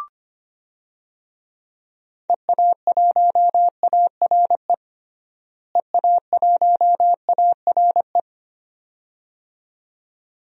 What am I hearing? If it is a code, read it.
Morse code, heard as a keyed tone, about 25 wpm, 700 Hz. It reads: EA1ARE EA1ARE
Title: DXCC-100 Call Signs - 6 Characters - Repeated 2x - 25wpm